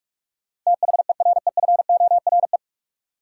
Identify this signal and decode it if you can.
Morse code, heard as a keyed tone, about 43 wpm, 700 Hz. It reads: THEREFORE
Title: Sentences from Top 800 Words - New Word Drill - 45wpm